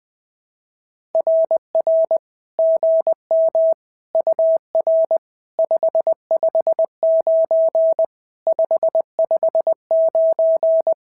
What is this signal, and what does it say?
Morse code, heard as a keyed tone, about 20 wpm, 650 Hz. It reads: RR GM UR 559 559
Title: SOTA QSO Practice 20wpm